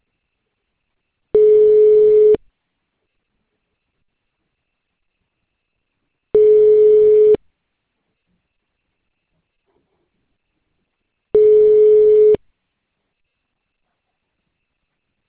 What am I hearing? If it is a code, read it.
Ringback tone (Europe, 425 Hz).